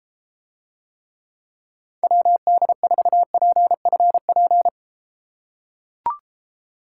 Morse code, heard as a keyed tone, about 31 wpm, 700 Hz. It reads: WD4PFP